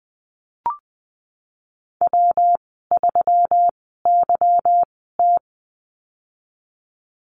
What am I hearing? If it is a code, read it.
Morse code, heard as a keyed tone, about 20 wpm, spaced as for 15 wpm, 700 Hz. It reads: W3YT
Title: 186 Lesson 30g - Letter b - Call Signs 20wpm